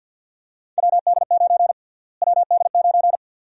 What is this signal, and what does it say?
Morse code, heard as a keyed tone, about 49 wpm, 700 Hz. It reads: WD9 WD9
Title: USA Call Sign Prefixes - Repeated 2x - 50wpm